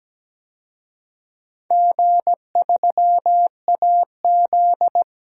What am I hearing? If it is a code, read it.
Morse code, heard as a keyed tone, about 17 wpm, 700 Hz. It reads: G3AZ
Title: DXCC-100 Call Signs - 4 Characters - 17wpm